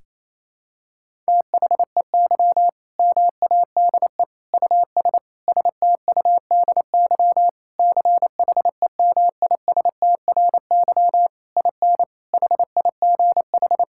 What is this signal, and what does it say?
Morse code, heard as a keyed tone, about 28 wpm, 700 Hz. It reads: THEY MADE US STUDY CHEMISTRY IN HIGH
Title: Sentences from Top 1000 Words - Review Part 6 - 28wpm